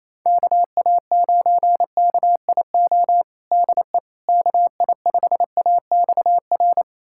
Morse code, heard as a keyed tone, about 28 wpm, 700 Hz. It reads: KA9KIO DE KI5AXR
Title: USA Callsign DE Callsign - 6 Characters - 28wpm